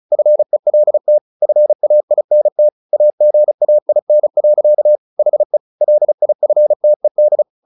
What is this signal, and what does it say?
Morse code, heard as a keyed tone, about 34 wpm, 600 Hz. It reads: FELT FAINT AGAIN. HE LIFTED
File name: the old man and the sea 2 35wpm